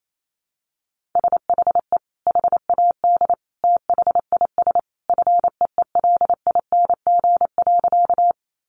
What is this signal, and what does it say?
Morse code, heard as a keyed tone, about 28 wpm, 700 Hz. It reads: SHE HAD THIS FEELING.
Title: Sentences from Top 1200 Words 28wpm